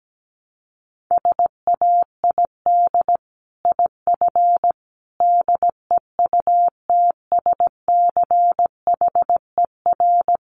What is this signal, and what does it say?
Morse code, heard as a keyed tone, about 17 wpm, 700 Hz. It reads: SAID IF DEUTSCHER
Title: A Sound Of Thunder 17wpm